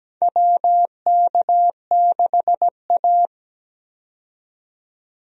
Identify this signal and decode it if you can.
Morse code, heard as a keyed tone, about 17 wpm, 700 Hz. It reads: WK6A
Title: USA Call Signs - 4 Characters - 17wpm